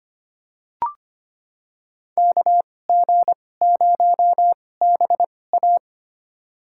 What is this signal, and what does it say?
Morse code, heard as a keyed tone, about 25 wpm, spaced as for 18 wpm, 700 Hz. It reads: KG0BA
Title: USA Call Signs - 5 Characters - 2x Character Spacing - 25wpm